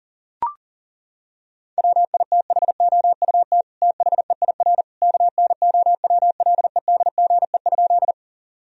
Morse code, heard as a keyed tone, about 40 wpm, 700 Hz. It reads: WITHOUT THEIR KNOWLEDGE?
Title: Sentences from Top 1000 Words - Review Part 3 - 40wpm